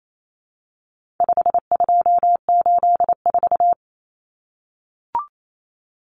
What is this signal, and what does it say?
Morse code, heard as a keyed tone, about 28 wpm, 700 Hz. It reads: 5284
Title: Common Cut Numbers - 4-Digit 28wpm